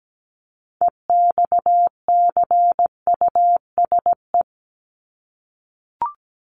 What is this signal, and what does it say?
Morse code, heard as a keyed tone, about 17 wpm, 700 Hz. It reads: EXCUSE
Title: Top 501-750 Words 17wpm